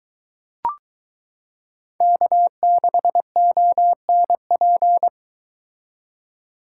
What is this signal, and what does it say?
Morse code, heard as a keyed tone, about 22 wpm, 700 Hz. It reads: K6ONP